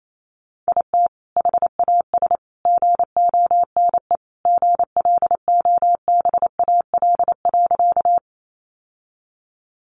Morse code, heard as a keyed tone, about 28 wpm, 700 Hz. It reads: IT HAS GONE GLOBAL.
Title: Sentences from Top 1200 Words 28wpm